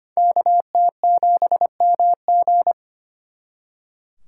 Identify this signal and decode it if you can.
Morse code, heard as a keyed tone, about 33 wpm, 700 Hz. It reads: KT7MG